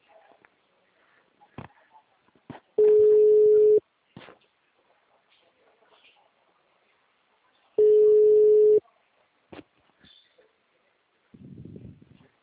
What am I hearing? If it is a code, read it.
Ringback tone (Europe, 425 Hz).